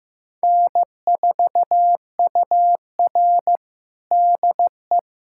Morse code, heard as a keyed tone, about 15 wpm, 700 Hz. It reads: N4UR DE
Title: USA Callsign DE Callsign - 4 Characters - 15wpm